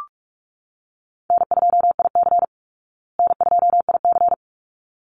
Morse code, heard as a keyed tone, about 44 wpm, 700 Hz. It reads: N2IC N2IC
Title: Easy Call Signs Repeated 2x - 45wpm